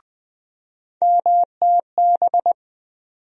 Morse code, heard as a keyed tone, about 20 wpm, 700 Hz. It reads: MTB
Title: USA Call Sign Postfixes - 3 Letters 20wpm